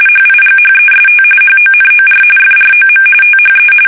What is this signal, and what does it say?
RTTY, about 45.45 baud. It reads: HIGHEST HONOR FROM THE
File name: fsk850 / benjamin-g-lamme